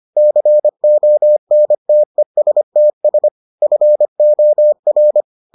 Morse code, heard as a keyed tone, about 25 wpm, 600 Hz. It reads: CONTESTS FOR